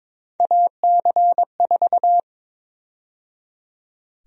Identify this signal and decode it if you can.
Morse code, heard as a keyed tone, about 22 wpm, 700 Hz. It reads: AC4
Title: USA Call Sign Prefixes - 22wpm